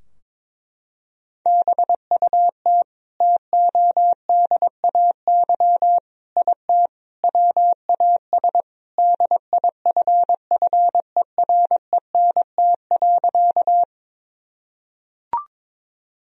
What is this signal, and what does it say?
Morse code, heard as a keyed tone, about 22 wpm, 700 Hz. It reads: BUT TODAY IT WAS DIFFERENT.
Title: Sentences from Top 1000 Words - Review Part 4 - 22wpm